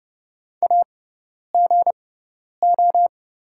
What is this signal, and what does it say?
Morse code, heard as a keyed tone, about 30 wpm, spaced as for 10 wpm, 690 Hz. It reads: AGO
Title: 6x Character Spacing - Top 500 Words - 30wpm